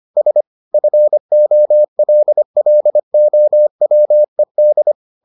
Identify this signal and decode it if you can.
Morse code, heard as a keyed tone, about 25 wpm, 600 Hz. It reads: S FOLLOWED